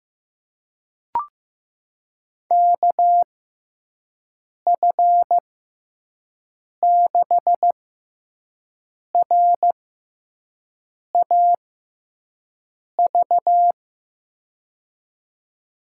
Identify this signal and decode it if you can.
Morse code, heard as a keyed tone, about 15 wpm, spaced as for 5 wpm, 700 Hz. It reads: KF6RAV
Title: USA Call Signs - 6 Characters - 6x Character Spacing - 15wpm